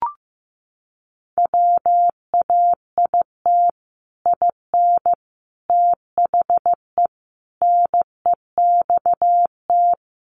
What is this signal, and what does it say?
Morse code, heard as a keyed tone, about 15 wpm, 700 Hz. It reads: WAIT IN THE NEXT
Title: Sentences from Top 1500 Words 15wpm